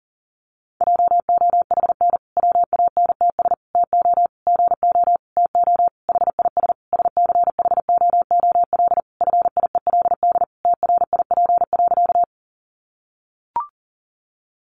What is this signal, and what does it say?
Morse code, heard as a keyed tone, about 40 wpm, 700 Hz. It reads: JOHN WANTS TO GO TO HIS SCHOOL FIELD TRIP.
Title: Additional Easy Phrases 40wpm